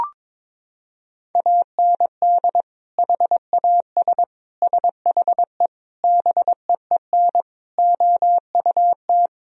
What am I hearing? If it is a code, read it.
Morse code, heard as a keyed tone, about 22 wpm, 700 Hz. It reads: AND HAS SHE BEEN OUT
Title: Sentences from Top 1000 Words - Review Part 1 - 22wpm